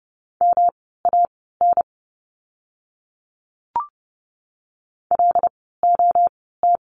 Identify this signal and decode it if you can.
Morse code, heard as a keyed tone, about 30 wpm, spaced as for 17 wpm, 700 Hz. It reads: MAN LOT
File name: course / 30:10wpm / 115 Lesson 19f - Letter w - Words 30wpm